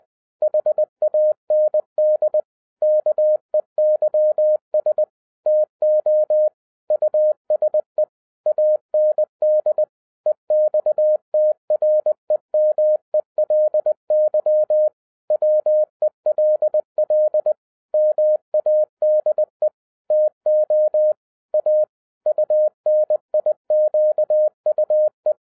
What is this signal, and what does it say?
Morse code, heard as a keyed tone, about 20 wpm, 610 Hz. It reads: HAND KEYS TO USE AND EXTREMELY WELL MADE TO A UNIQUE
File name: 2011-07-mp3 / n43 Czech Army Morse Keys 20wpm